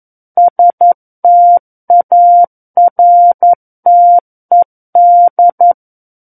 Morse code, heard as a keyed tone, about 11 wpm, 700 Hz. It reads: STARTED
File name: 11 WPM